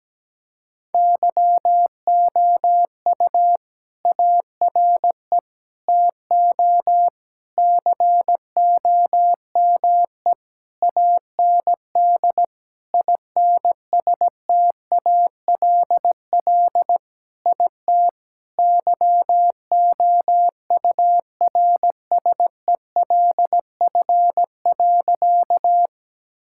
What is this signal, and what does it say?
Morse code, heard as a keyed tone, about 17 wpm, 700 Hz. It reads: YOU ARE TO COME AND INSTALL IT YOURSELF.
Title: Sentences from Top 1600 Words 17wpm